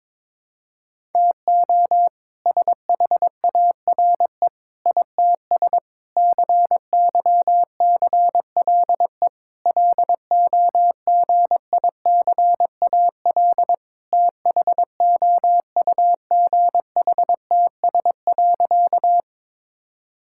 Morse code, heard as a keyed tone, about 22 wpm, 700 Hz. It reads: TO SHARE ITS CYCLE LOGICAL THOUGHTS.
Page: morse-code-ninja-practice-22wpm